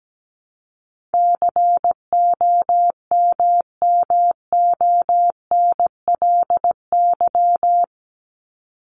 Morse code, heard as a keyed tone, about 17 wpm, 700 Hz. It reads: COMMONLY
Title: LY Suffix - Top 100 Words with Base Words 17wpm